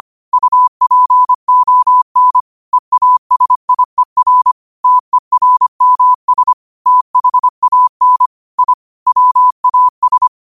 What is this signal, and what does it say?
Morse code, heard as a keyed tone, about 25 wpm, 1000 Hz. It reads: UPON EASIER TERMS THAN I WAS
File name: qotdfiles / 25wpm